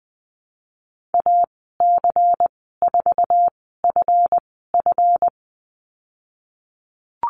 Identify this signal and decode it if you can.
Morse code, heard as a keyed tone, about 20 wpm, spaced as for 15 wpm, 700 Hz. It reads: AC4FF
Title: 167 Lesson 27g - Number 7 - Call Signs 20wpm